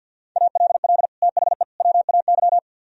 Morse code, heard as a keyed tone, about 62 wpm, 700 Hz. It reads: ALL THE WAY
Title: Speed-Racing - Sets of 3 Words 50wpm